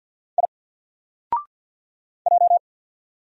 Morse code, heard as a keyed tone, about 48 wpm, 700 Hz. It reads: I J
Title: Single Letter-Number Alt 50wpm